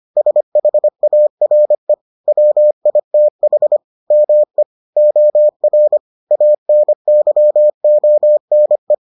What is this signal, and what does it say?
Morse code, heard as a keyed tone, about 25 wpm, 600 Hz. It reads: SHARE WITH ME OR ANYONE